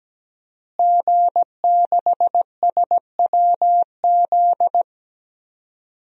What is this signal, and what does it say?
Morse code, heard as a keyed tone, about 17 wpm, 700 Hz. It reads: G6SWZ
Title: DXCC-100 Call Signs - 5 Characters - Repeated 2x - 17wpm